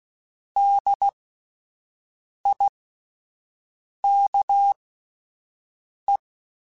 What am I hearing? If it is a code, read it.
Morse code, heard as a keyed tone, about 16 wpm, spaced as for 5 wpm, 800 Hz. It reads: DIKE